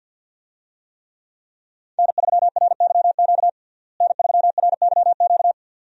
Morse code, heard as a keyed tone, about 51 wpm, 700 Hz. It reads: N3RYQ N3RYQ
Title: USA Call Signs - 5 Characters - Repeated 2x - 50wpm